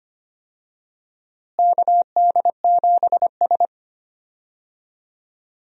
Morse code, heard as a keyed tone, about 25 wpm, 700 Hz. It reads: KD7S
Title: USA Call Signs - 4 Characters - Repeated 3x - 25wpm